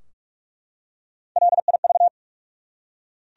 Morse code, heard as a keyed tone, about 46 wpm, 700 Hz. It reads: LIV
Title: USA Common Names - Max 4 Letters - 45wpm